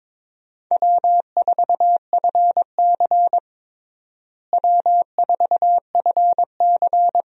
Morse code, heard as a keyed tone, about 22 wpm, 700 Hz. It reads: W4FC W4FC
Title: USA Call Signs - 4 Characters - Repeated 2x - 22wpm